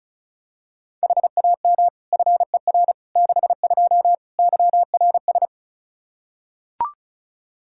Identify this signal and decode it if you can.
Morse code, heard as a keyed tone, about 36 wpm, 700 Hz. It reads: HAM FER 62 YRS
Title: Practice QSO - Exchange 3 - More Info - 35wpm